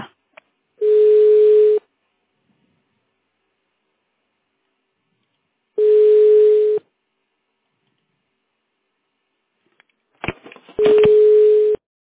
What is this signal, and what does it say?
Ringback tone (Europe, 425 Hz).